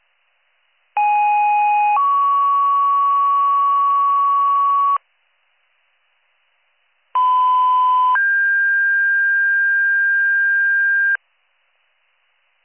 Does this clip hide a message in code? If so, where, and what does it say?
Yes: two-tone page from 1.0 to 11.2 s, heard as a one-second tone, then a three-second tone, about 830 then 1120 Hz, about 1010 then 1600 Hz.